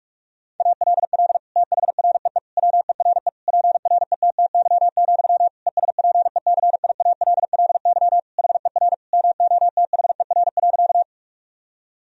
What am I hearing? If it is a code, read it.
Morse code, heard as a keyed tone, about 46 wpm, 700 Hz. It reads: ALL THREE WERE PRETTY, ESPECIALLY HER MOTHER.
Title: Sentences from Top 1000 Words - Review Part 4 - 45wpm